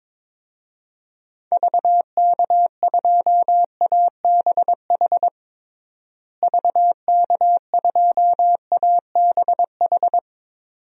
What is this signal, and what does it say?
Morse code, heard as a keyed tone, about 22 wpm, 700 Hz. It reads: VK2ABH VK2ABH
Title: DXCC-100 Call Signs - 6 Characters - Repeated 2x - 22wpm